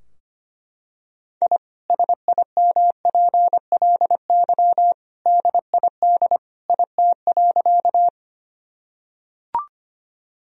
Morse code, heard as a keyed tone, about 25 wpm, 700 Hz. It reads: I SIMPLY DID IT.
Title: Sentences from Top 1000 Words - Review Part 4 - 25wpm